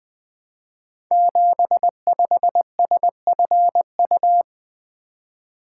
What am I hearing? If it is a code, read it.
Morse code, heard as a keyed tone, about 20 wpm, 700 Hz. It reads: 75SFU
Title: Code Groups - 5 Letter:Numbers - 20wpm